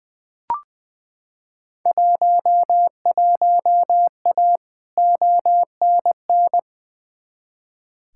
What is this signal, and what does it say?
Morse code, heard as a keyed tone, about 20 wpm, 700 Hz. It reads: 11A ONN
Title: Field Day Exchanges Encore - 20wpm